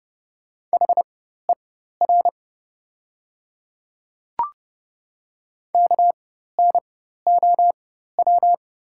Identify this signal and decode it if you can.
Morse code, heard as a keyed tone, about 30 wpm, spaced as for 14 wpm, 700 Hz. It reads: HER KNOW
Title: 4x Character Spacing - Top 300 Words - 30wpm